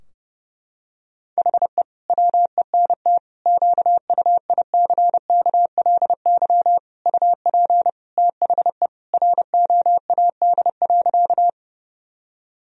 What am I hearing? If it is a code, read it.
Morse code, heard as a keyed tone, about 32 wpm, 700 Hz. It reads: HE WENT QUICKLY UP THE ROAD.
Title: Sentences from Top 1000 Words - Review Part 2 - 30wpm